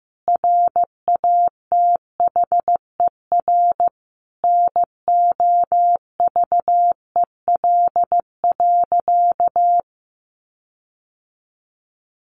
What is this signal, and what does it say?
Morse code, heard as a keyed tone, about 15 wpm, 700 Hz. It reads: RATHER NOVEL.
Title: Sentences from Top 1300 Words 15wpm